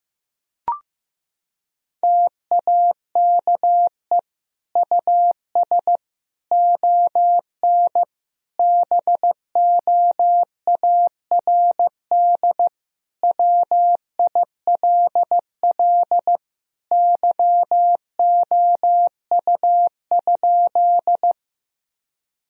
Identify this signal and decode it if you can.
Morse code, heard as a keyed tone, about 15 wpm, 700 Hz. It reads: TAKE US ON BOARD WILL YOU?
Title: Sentences from Top 1000 Words - Review Part 4 - 15wpm